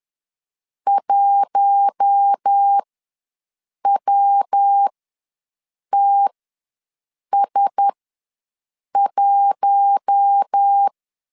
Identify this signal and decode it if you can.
Morse code, heard as a keyed tone, about 10 wpm, spaced as for 6 wpm, 800 Hz. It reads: 1WTS1
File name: LEZIONI / 5 LBVC / 5 LBVC ws 10wpm 3733